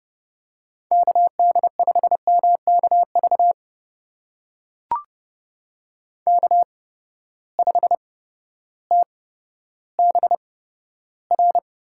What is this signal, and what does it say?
Morse code, heard as a keyed tone, about 30 wpm, spaced as for 14 wpm, 700 Hz. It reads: KD5MKV K5TBR
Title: USA Call Signs - 8x Character Spacing - 30wpm